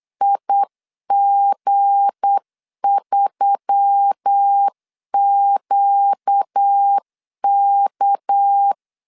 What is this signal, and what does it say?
Morse code, heard as a keyed tone, about 8 wpm, 800 Hz. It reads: IG3QK
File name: LEZIONI / 7 GF9Z / 7 GF 8wpm 3108